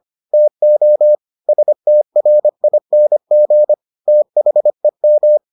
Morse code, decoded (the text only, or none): TO STRING THEM